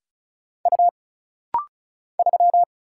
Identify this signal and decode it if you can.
Morse code, heard as a keyed tone, about 34 wpm, 700 Hz. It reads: U 3
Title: Single Letter-Number - Rapid-Fire 35wpm